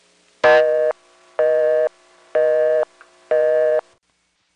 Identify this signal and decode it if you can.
Busy tone (North America), 480 + 620 Hz.